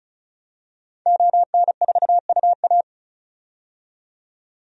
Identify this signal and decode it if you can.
Morse code, heard as a keyed tone, about 36 wpm, 700 Hz. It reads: ON4UA
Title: DXCC-100 Call Signs - 5 Characters - 35wpm